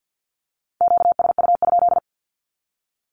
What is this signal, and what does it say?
Morse code, heard as a keyed tone, about 50 wpm, 700 Hz. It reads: QSU?
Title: Q-Codes with Spoken Meaning - 50wpm